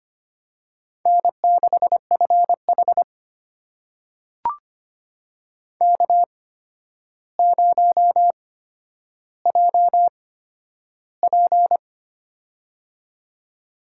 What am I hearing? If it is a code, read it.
Morse code, heard as a keyed tone, about 25 wpm, spaced as for 11 wpm, 700 Hz. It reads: N6FH K0JP
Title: USA Call Signs - 4 Characters - 8x Character Spacing - 25wpm